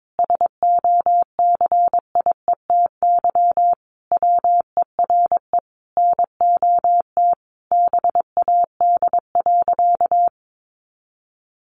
Morse code, heard as a keyed tone, about 22 wpm, 700 Hz. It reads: SOCIETY WERE NOT BAD.
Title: Sentences from Top 900 Words 22wpm